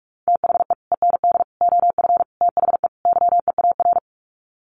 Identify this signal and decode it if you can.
Morse code, heard as a keyed tone, about 45 wpm, 700 Hz. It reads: THE END OF THE YEAR
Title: Sets of 5 Words 45wpm